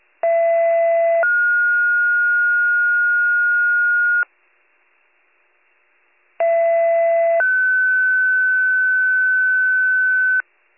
Two-tone page, heard as a one-second tone, then a three-second tone, about 670 then 1430 Hz, about 670 then 1500 Hz. It is heard from 0.2 to 10.4 s.